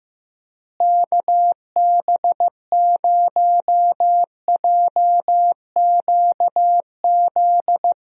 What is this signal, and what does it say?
Morse code, heard as a keyed tone, about 15 wpm, 700 Hz. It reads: KB0JQZ
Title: USA Call Signs - 6 Characters - Repeated 3x - 15wpm